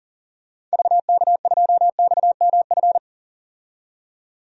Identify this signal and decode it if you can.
Morse code, heard as a keyed tone, about 40 wpm, 700 Hz. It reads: VK2XMF
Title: DXCC-100 Call Signs - 6 Characters - 40wpm